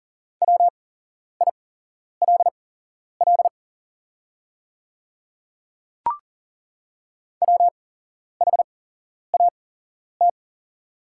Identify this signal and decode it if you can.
Morse code, heard as a keyed tone, about 40 wpm, spaced as for 11 wpm, 700 Hz. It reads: WILL WHAT
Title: 8x Character Spacing - Top 100 Words - 40wpm